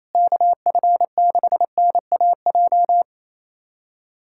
Morse code, heard as a keyed tone, about 27 wpm, 700 Hz. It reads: KF6NAJ
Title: USA Callsign DE Callsign - 6 Characters - 28wpm